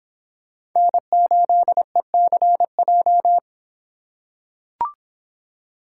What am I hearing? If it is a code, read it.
Morse code, heard as a keyed tone, about 26 wpm, 700 Hz. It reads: N8ECJ